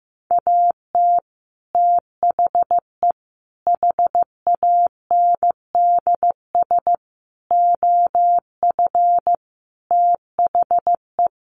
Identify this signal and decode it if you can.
Morse code, heard as a keyed tone, about 14 wpm, 700 Hz. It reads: AT THE HANDS OF THE
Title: Sets of 5 Words Encore2 15wpm